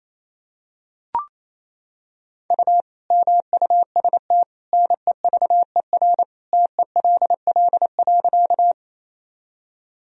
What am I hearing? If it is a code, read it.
Morse code, heard as a keyed tone, about 28 wpm, 700 Hz. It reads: U MUST NEVER TELL.
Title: Sentences from Top 300 Words and Abbreviations 28wpm